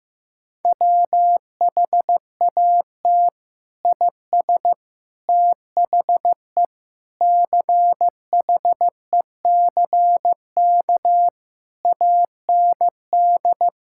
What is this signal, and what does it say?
Morse code, heard as a keyed tone, about 15 wpm, 700 Hz. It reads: WHAT IS THE CHECK AND
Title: Sentences from Top 1000 Words - Review Part 6 - 15wpm